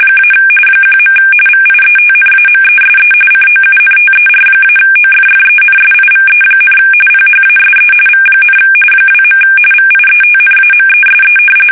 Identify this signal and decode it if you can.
RTTY, about 45.45 baud. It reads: - 1. VIETNAMESE COUNTER-OFFENSIVE, PHASE V, 2. VIETNAMESE
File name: fsk850 / battleshipnj